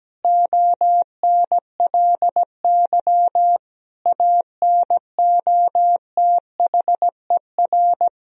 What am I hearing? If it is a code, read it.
Morse code, heard as a keyed tone, about 17 wpm, 700 Hz. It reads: ONLY ANOTHER
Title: Sentences from Top 1000 Words - Review Part 1 - 17wpm